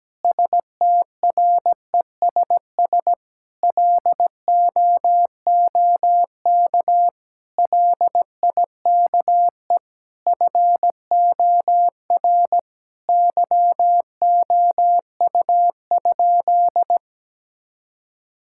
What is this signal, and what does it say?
Morse code, heard as a keyed tone, about 17 wpm, 700 Hz. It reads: STRESS LOOK LIKE FOR YOU?
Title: Sentences from Top 1000 Words - Review Part 6 - 17wpm